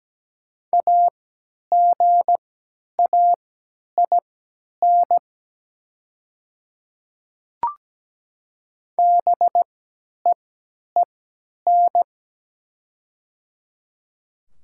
Morse code, heard as a keyed tone, about 17 wpm, spaced as for 10 wpm, 700 Hz. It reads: AGAIN BEEN
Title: 3x Character Spacing - Top 200 Words - 17wpm